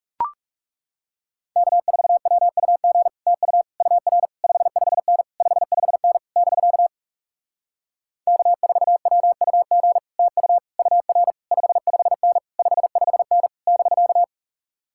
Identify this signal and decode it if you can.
Morse code, heard as a keyed tone, about 37 wpm, 700 Hz. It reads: K4WUG TU UR 55N 55N <BK> K4WUG TU UR 55N 55N <BK>
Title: Speed-Racing Common QSO Phrases - 30wpm